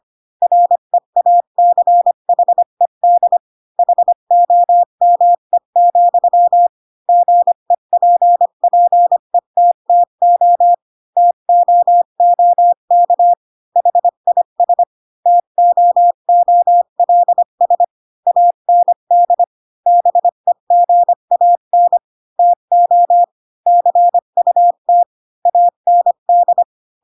Morse code, heard as a keyed tone, about 25 wpm, 700 Hz. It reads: REACHED HOME, GEPPETTO TOOK HIS TOOLS AND BEGAN TO CUT AND
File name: Pinocchio / Pinocchio ch3 20 25 ik1hge